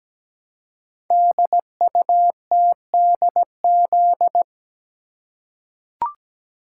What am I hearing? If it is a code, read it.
Morse code, heard as a keyed tone, about 17 wpm, 700 Hz. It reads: DUTDZ
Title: Code Groups - 5 Letters - 17wpm